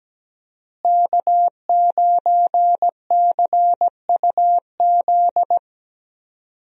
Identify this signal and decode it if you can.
Morse code, heard as a keyed tone, about 17 wpm, 700 Hz. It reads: K9CUZ
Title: USA Call Signs - 5 Characters - Repeated 2x - 17wpm